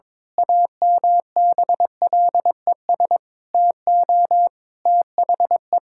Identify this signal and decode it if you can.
Morse code, heard as a keyed tone, about 21 wpm, 700 Hz. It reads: AMBLES TO THE
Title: Pangram Challenge v2 - 22wpm